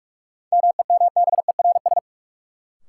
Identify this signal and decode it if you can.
Morse code, heard as a keyed tone, about 46 wpm, 700 Hz. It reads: MEMBERS
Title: Sets of 3 Words - New Word Drill - 45wpm